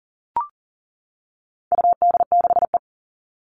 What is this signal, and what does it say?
Morse code, heard as a keyed tone, about 40 wpm, 700 Hz. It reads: UD6E
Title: All Cut Numbers - 4-Digit 40wpm